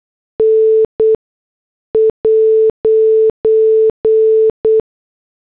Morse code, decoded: N'